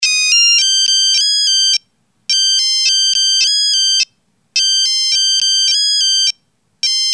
There is no code.